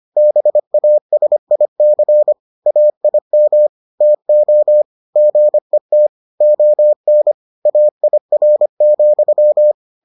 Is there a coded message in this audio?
Yes: Morse code from 0.2 to 9.7 s, heard as a keyed tone, about 25 wpm, 600 Hz. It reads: BASIC AIM TO GET ON AIR,